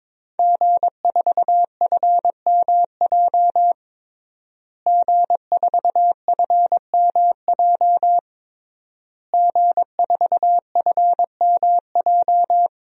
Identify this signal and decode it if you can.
Morse code, heard as a keyed tone, about 22 wpm, 700 Hz. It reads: G4FMJ G4FMJ G4FMJ
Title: DXCC-100 Call Signs - 5 Characters - Repeated 3x - 22wpm